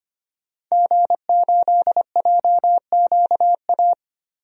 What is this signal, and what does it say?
Morse code, heard as a keyed tone, about 25 wpm, 700 Hz. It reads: G8JQA
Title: DXCC-100 Call Signs - 5 Characters - Repeated 3x - 25wpm